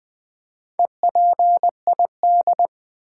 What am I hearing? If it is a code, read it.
Morse code, heard as a keyed tone, about 20 wpm, 700 Hz. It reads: EPID
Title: Build A Word - 4 Letters At A Time - 8-Letter Words - 20wpm